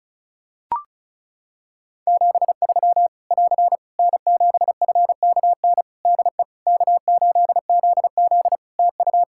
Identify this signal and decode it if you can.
Morse code, heard as a keyed tone, about 36 wpm, 700 Hz. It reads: 73 <AR> N7FKN DE K8ZZ TU
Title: Practice QSO - Exchange 4 - Ending - 35wpm